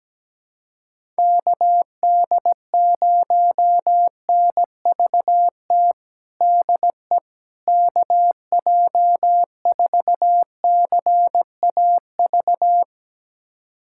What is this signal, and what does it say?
Morse code, heard as a keyed tone, about 17 wpm, 700 Hz. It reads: KD0NVT DE KJ4CAV
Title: USA Callsign DE Callsign 17wpm